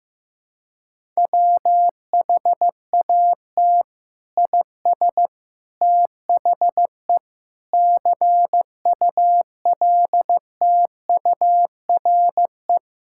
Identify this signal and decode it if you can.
Morse code, heard as a keyed tone, about 15 wpm, 700 Hz. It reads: WHAT IS THE CULTURE
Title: Sentences from Top 1000 Words - Review Part 6 - 15wpm